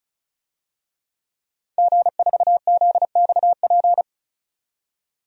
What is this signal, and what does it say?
Morse code, heard as a keyed tone, about 36 wpm, 700 Hz. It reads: G4ZXP
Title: DXCC-100 Call Signs - 5 Characters - 35wpm